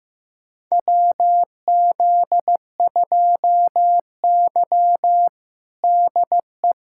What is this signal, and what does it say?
Morse code, heard as a keyed tone, about 16 wpm, 700 Hz. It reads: WZ2Y DE
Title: USA Callsign DE Callsign - 4 Characters - 15wpm